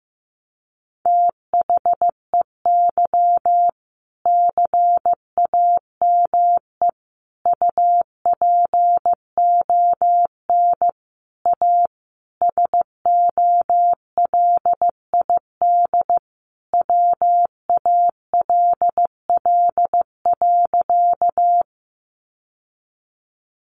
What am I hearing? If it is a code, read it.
Morse code, heard as a keyed tone, about 15 wpm, 700 Hz. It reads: THEY CAME UPON A SOLID WALL.
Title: Sentences from Top 1500 Words 15wpm